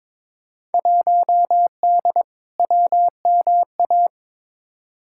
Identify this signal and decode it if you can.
Morse code, heard as a keyed tone, about 23 wpm, 700 Hz. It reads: 1D WMA
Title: Field Day Exchanges - 22wpm